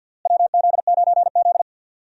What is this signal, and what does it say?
Morse code, heard as a keyed tone, about 53 wpm, 700 Hz. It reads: WZ9Z
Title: USA Call Signs - 4 Characters - Repeated 3x - 50wpm